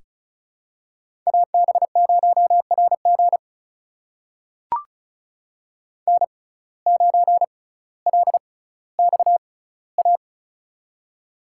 Morse code, heard as a keyed tone, about 36 wpm, spaced as for 18 wpm, 700 Hz. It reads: AB0RG N9LXA
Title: USA Call Signs - 5 Characters - 6x Character Spacing - 35wpm